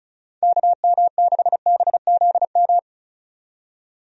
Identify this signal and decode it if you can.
Morse code, heard as a keyed tone, about 36 wpm, 700 Hz. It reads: KM6BZM
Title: USA Call Signs - 6 Characters - 35wpm